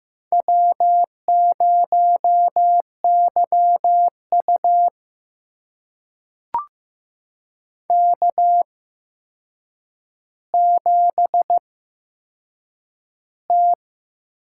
Morse code, heard as a keyed tone, about 15 wpm, spaced as for 7 wpm, 700 Hz. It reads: W0YU K7T
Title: USA Call Signs - 4 Characters - 8x Character Spacing - 15wpm